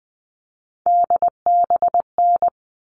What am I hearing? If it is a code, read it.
Morse code, heard as a keyed tone, about 20 wpm, 700 Hz. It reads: DBN
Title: All Cut Numbers - 3-Digit 20wpm